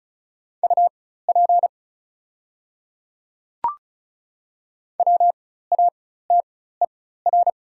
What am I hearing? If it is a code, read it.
Morse code, heard as a keyed tone, about 36 wpm, spaced as for 16 wpm, 700 Hz. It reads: UP WATER
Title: 4x Character Spacing - Top 300 Words - 35wpm